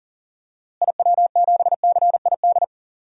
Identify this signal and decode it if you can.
Morse code, heard as a keyed tone, about 40 wpm, 700 Hz. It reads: IW7CID
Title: DXCC-100 Call Signs - 6 Characters - Repeated 2x - 40wpm